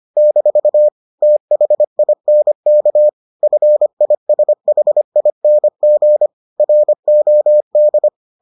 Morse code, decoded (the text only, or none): - THINK FISHING ROD